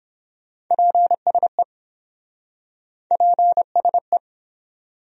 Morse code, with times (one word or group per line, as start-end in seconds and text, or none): 0.7-1.6 PSE
3.1-4.2 PSE